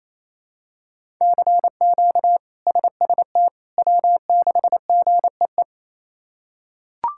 Morse code, heard as a keyed tone, about 27 wpm, 700 Hz. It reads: CQ SST W6GEE
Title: K1USN SST QSO Practice - 28wpm